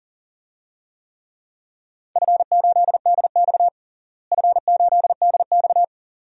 Morse code, heard as a keyed tone, about 40 wpm, 700 Hz. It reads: F8DX F8DX
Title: DXCC-100 Call Signs - 4 Characters - Repeated 2x - 40wpm